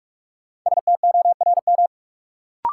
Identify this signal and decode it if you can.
Morse code, heard as a keyed tone, about 46 wpm, 700 Hz. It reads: STORM
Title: Sentences from Top 1500 Words - New Word Drill - 45wpm